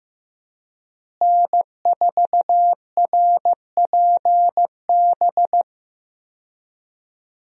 Morse code, heard as a keyed tone, about 15 wpm, 700 Hz. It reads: N4RPB
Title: USA Call Signs - 5 Characters - Repeated 3x - 15wpm